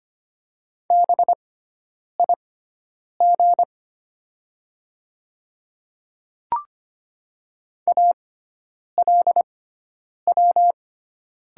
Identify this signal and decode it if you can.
Morse code, heard as a keyed tone, about 25 wpm, spaced as for 9 wpm, 690 Hz. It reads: BIG ALW
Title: 6x Character Spacing - Top 300 Words - 25wpm